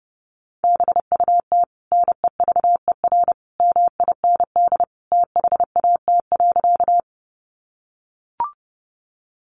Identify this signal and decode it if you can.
Morse code, heard as a keyed tone, about 30 wpm, 700 Hz. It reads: BUT NEVER MIND THAT.
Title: Sentences from Top 500 Words 30wpm